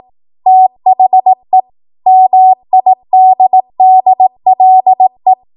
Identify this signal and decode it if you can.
Morse code, heard as a keyed tone, about 18 wpm, 750 Hz. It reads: THE MIDDLE
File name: Archive / 18 WPM / 18WPM